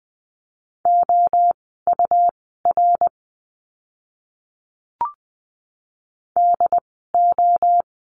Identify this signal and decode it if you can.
Morse code, heard as a keyed tone, about 20 wpm, spaced as for 15 wpm, 700 Hz. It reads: OUR DO
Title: 237 Lesson 38f - Letter q - Words 20wpm